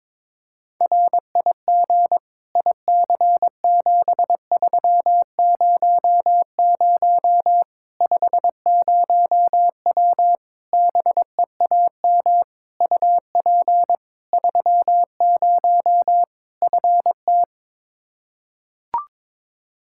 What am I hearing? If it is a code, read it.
Morse code, heard as a keyed tone, about 22 wpm, 700 Hz. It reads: RIG IC7300 50W BEAM UP 30 FT
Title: Practice QSO - Exchange 2 - Info - 22wpm